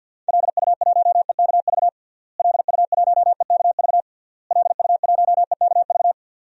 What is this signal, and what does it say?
Morse code, heard as a keyed tone, about 52 wpm, 700 Hz. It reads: LU1EKV LU1EKV LU1EKV
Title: DXCC-100 Call Signs - 6 Characters - Repeated 3x - 50wpm